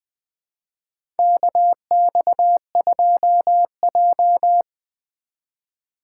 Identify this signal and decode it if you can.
Morse code, heard as a keyed tone, about 21 wpm, 700 Hz. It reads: KX2J
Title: USA Call Signs - 2x Character Spacing - 20wpm